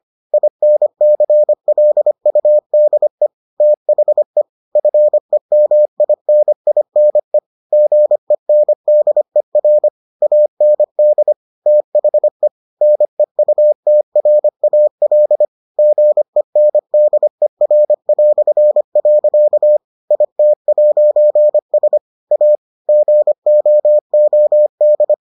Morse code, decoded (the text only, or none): INCLUDE THE FEMININE GENDER AND THE NEUTRAL GENDER". IT'S A GOOD